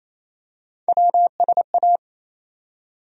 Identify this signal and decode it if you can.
Morse code, heard as a keyed tone, about 28 wpm, 700 Hz. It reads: WSA
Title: USA Call Sign Postfixes - 3 Letters 28wpm